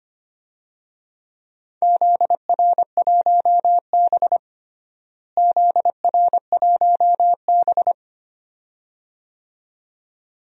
Morse code, heard as a keyed tone, about 25 wpm, 700 Hz. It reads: ZR1B ZR1B
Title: DXCC-100 Call Signs - 4 Characters - Repeated 2x - 25wpm